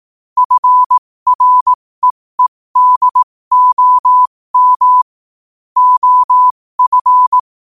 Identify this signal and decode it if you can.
Morse code, heard as a keyed tone, about 18 wpm, spaced as for 15 wpm, 1000 Hz. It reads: FREEDOM OF
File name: qotdfiles / 15wpm